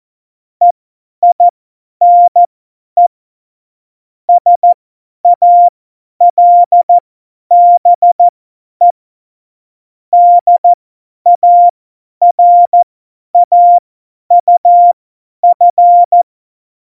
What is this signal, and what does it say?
Morse code, heard as a keyed tone, about 14 wpm, spaced as for 10 wpm, 700 Hz. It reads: EINE SALBE DARAUF
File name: Morsekurs / Witze / Witz 06 March 2018, 50 F70